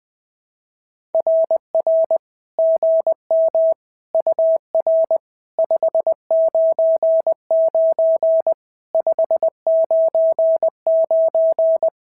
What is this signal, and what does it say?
Morse code, heard as a keyed tone, about 17 wpm, 650 Hz. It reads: RR GM UR 599 599
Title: SOTA QSO Practice 20wpm